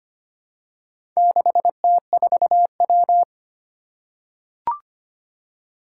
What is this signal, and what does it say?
Morse code, heard as a keyed tone, about 25 wpm, 700 Hz. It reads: 6T4W
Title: Four Challenging Letter-Number Combinations 25wpm